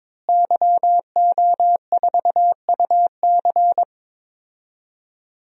Morse code, heard as a keyed tone, about 22 wpm, 700 Hz. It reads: YO4UC
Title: DXCC-100 Call Signs - 5 Characters - Repeated 2x - 22wpm